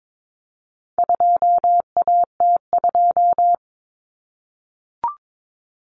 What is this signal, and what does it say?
Morse code, heard as a keyed tone, about 22 wpm, 700 Hz. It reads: 2AT2
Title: Common Cut Numbers - 4-Digit 22wpm